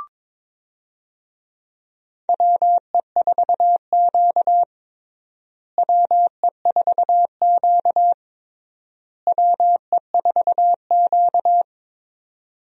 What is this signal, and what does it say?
Morse code, heard as a keyed tone, about 22 wpm, 700 Hz. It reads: WE4Q WE4Q WE4Q
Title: USA Call Signs - 4 Characters - Repeated 3x - 22wpm